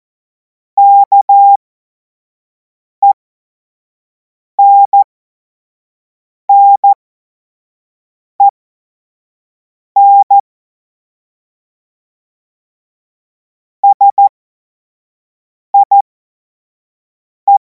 Morse code, heard as a keyed tone, about 14 wpm, spaced as for 5 wpm, 800 Hz. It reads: KENNEN SIE
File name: Morsekurs / Witze / Witz 07. April 2026 25 F70